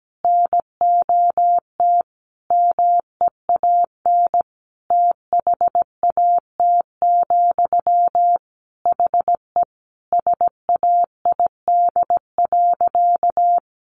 Morse code, heard as a keyed tone, about 17 wpm, 700 Hz. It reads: NOT MEAN THAT, HE SAID.
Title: Sentences from Top 600 Words 17wpm